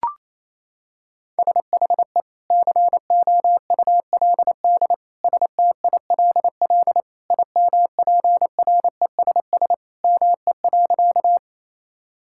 Morse code, heard as a keyed tone, about 28 wpm, 700 Hz. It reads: SHE COULD STILL IMPRESS ME.
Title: Sentences from Top 1600 Words 28wpm